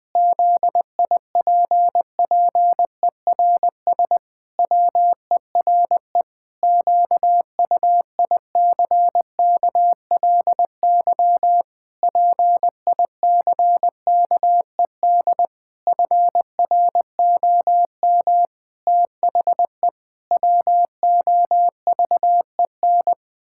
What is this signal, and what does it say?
Morse code, heard as a keyed tone, about 20 wpm, 700 Hz. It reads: ZIPPERS WERE QUICKLY PICKED FROM THE WOVEN
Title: Pangram Challenge v2 - 20wpm